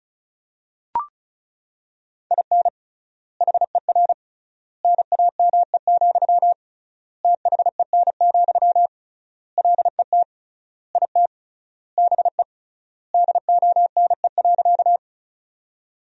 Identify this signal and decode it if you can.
Morse code, heard as a keyed tone, about 36 wpm, 700 Hz. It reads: IN HER NAME, THEN, LET IT BE DONE.
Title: 3x Word-Spacing - Sentences from Top 500 Words - 35wpm